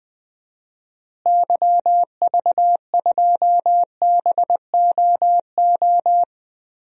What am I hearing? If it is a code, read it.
Morse code, heard as a keyed tone, about 20 wpm, 700 Hz. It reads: YV2BOO
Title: DXCC-100 Call Signs - 6 Characters - Repeated 2x - 20wpm